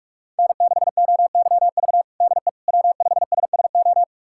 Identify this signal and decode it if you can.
Morse code, heard as a keyed tone, about 46 wpm, 700 Hz. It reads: N6OYV DE W5SSO
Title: USA Callsign DE Callsign - 5 Characters - 45wpm